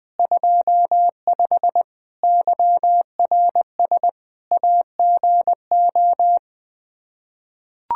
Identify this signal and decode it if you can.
Morse code, heard as a keyed tone, about 21 wpm, 700 Hz. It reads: 25 YRS AGO
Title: Practice QSO - Exchange 3 - More Info - 20wpm